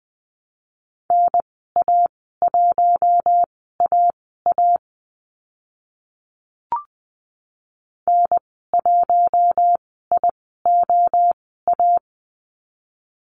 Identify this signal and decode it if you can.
Morse code, heard as a keyed tone, about 20 wpm, spaced as for 14 wpm, 700 Hz. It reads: NA1AA N1IOA
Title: 042 Lesson 8g - Number 1 - Call Signs 20wpm